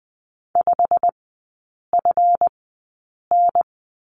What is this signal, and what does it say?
Morse code, heard as a keyed tone, about 20 wpm, spaced as for 8 wpm, 700 Hz. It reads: 5FN
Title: 152 Lesson 25e - Letter p - Three Characters 20wpm